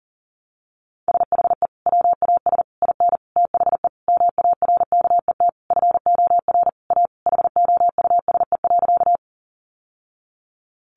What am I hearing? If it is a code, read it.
Morse code, heard as a keyed tone, about 40 wpm, 700 Hz. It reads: SHE WAS IN THE MARKET FOR A HOUSE.
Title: Sentences from Top 600 Words 40wpm